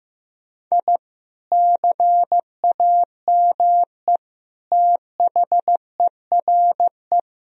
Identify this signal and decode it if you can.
Morse code, heard as a keyed tone, about 15 wpm, 700 Hz. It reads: I CAME THERE
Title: Sentences from Top 1000 Words - Review Part 4 - 15wpm